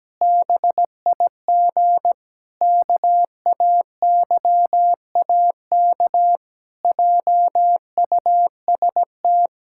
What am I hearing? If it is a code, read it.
Morse code, heard as a keyed tone, about 17 wpm, 700 Hz. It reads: BIG KAYAK JUST
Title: Pangram Challenge v2 - 17wpm